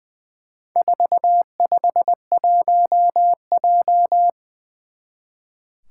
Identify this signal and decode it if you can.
Morse code, heard as a keyed tone, about 20 wpm, 700 Hz. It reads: 451J
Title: Four Challenging Letter-Number Combinations 20wpm